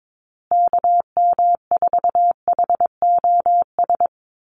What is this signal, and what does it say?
Morse code, heard as a keyed tone, about 22 wpm, 700 Hz. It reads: KM4HOS
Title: USA Call Signs Repeated 3x 22wpm